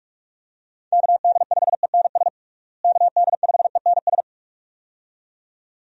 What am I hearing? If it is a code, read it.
Morse code, heard as a keyed tone, about 46 wpm, 700 Hz. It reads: KD5ENS KD5ENS
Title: USA Call Signs - 6 Characters - Repeated 2x - 45wpm